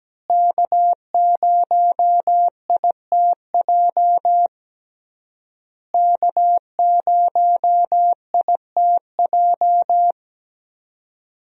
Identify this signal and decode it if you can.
Morse code, heard as a keyed tone, about 17 wpm, 700 Hz. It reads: K0ITJ K0ITJ
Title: USA Call Signs - 5 Characters - Repeated 3x - 17wpm